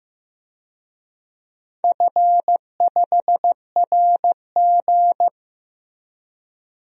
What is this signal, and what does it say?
Morse code, heard as a keyed tone, about 15 wpm, 700 Hz. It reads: F5RG
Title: DXCC-100 Call Signs - 4 Characters - Repeated 3x - 15wpm